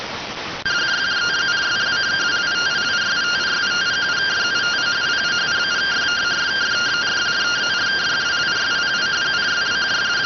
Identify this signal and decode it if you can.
Two-tone teleprinter signal, about 170 Hz shift.